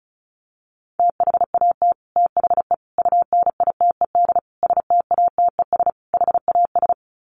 Morse code, heard as a keyed tone, about 34 wpm, 700 Hz. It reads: THAT THE UNITED STATES HAS
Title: Sets of 5 Words Encore 35wpm